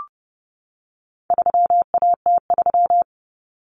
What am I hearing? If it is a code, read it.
Morse code, heard as a keyed tone, about 30 wpm, 700 Hz. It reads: 3AT3
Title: Common Cut Numbers - 4-Digit 30wpm